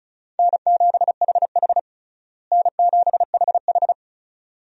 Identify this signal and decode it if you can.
Morse code, heard as a keyed tone, about 36 wpm, 700 Hz. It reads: N7HH N7HH
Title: USA Call Signs - 4 Characters - Repeated 2x - 35wpm